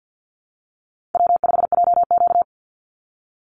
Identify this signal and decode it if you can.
Morse code, heard as a keyed tone, about 50 wpm, 700 Hz. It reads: W5JQ
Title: Easy Call Signs Repeated 2x - 50wpm